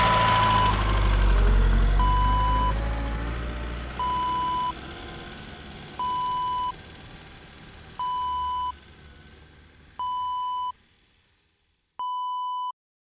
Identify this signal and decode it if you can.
There is no code.